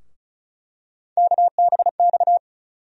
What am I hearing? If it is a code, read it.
Morse code, heard as a keyed tone, about 34 wpm, 700 Hz. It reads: KBX
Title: USA Call Sign Postfixes - 3 Letters 35wpm